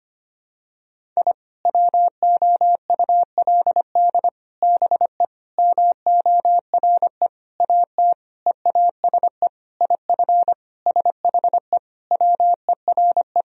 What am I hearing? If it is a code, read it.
Morse code, heard as a keyed tone, about 25 wpm, 700 Hz. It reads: I WOULD BE MORE AT EASE IF SHE WERE
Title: Sentences from Top 1000 Words - Review Part 1 - 25wpm